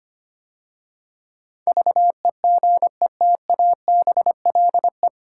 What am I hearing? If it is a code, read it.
Morse code, heard as a keyed tone, about 25 wpm, 700 Hz. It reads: VEGETABLE
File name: practice / 25wpm / 9 Letter Words - First 3 Letters Given - 25wpm